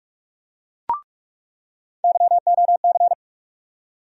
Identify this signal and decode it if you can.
Morse code, heard as a keyed tone, about 46 wpm, 700 Hz. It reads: YOC
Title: USA Call Sign Postfixes - 3 Letters 45wpm